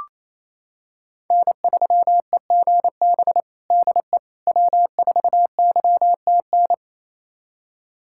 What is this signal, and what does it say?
Morse code, heard as a keyed tone, about 28 wpm, 700 Hz. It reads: N3EGB DE W4YTN
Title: USA Callsign DE Callsign - 5 Characters - 28wpm